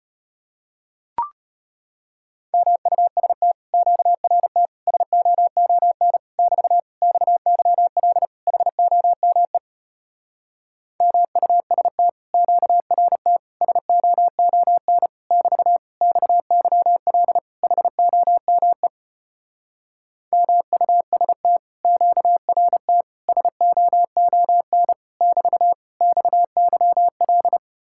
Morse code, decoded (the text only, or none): MUST QRT SOON <BT> XYL HOME MUST QRT SOON <BT> XYL HOME MUST QRT SOON <BT> XYL